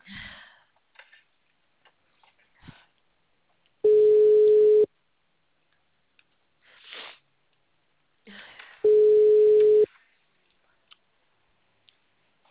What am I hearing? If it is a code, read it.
Ringback tone (Europe, 425 Hz).
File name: out- -2013- - - .0